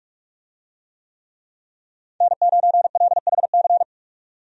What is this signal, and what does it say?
Morse code, heard as a keyed tone, about 46 wpm, 700 Hz. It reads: N9LHC
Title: USA Call Signs - 5 Characters - 45wpm